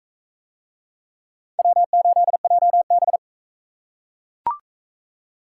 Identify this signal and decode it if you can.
Morse code, heard as a keyed tone, about 39 wpm, 700 Hz. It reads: W8JB